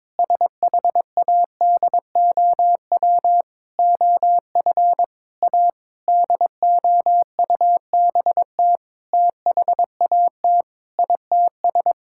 Morse code, heard as a keyed tone, about 22 wpm, 700 Hz. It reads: SHADOW OF A DOUBT THAT ITS
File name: Sentences with 7-Word Phrases - 22wpm